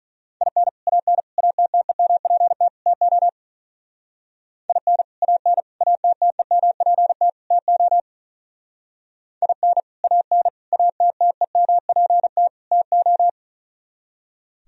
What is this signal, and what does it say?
Morse code, heard as a keyed tone, about 42 wpm, 700 Hz. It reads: IN AN ATTEMPT TO IN AN ATTEMPT TO IN AN ATTEMPT TO
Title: Speed-Racing - Sets of 4 Words 35wpm